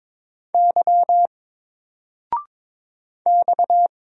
Morse code, heard as a keyed tone, about 22 wpm, 700 Hz. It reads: Y X
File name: Single-Letter-Number-22wpm